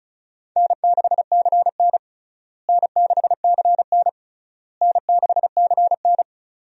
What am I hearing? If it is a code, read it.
Morse code, heard as a keyed tone, about 34 wpm, 700 Hz. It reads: N6CN N6CN N6CN
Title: USA Call Signs - 4 Characters - Repeated 3x - 35wpm